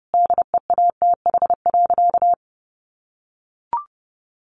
Morse code, heard as a keyed tone, about 30 wpm, 700 Hz. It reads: DEATH.
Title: Sentences from Top 1100 Words 30wpm